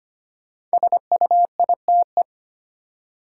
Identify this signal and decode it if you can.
Morse code, heard as a keyed tone, about 25 wpm, 700 Hz. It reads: SUITE
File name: 5 Letter Words - First 3 Letters Given - 25wpm